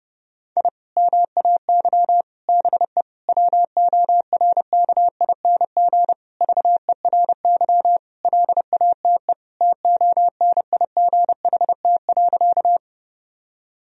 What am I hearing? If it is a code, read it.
Morse code, heard as a keyed tone, about 32 wpm, 700 Hz. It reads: I MAY BE WORKING VERY LATE TONIGHT.
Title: Sentences from Top 1000 Words - Review Part 4 - 30wpm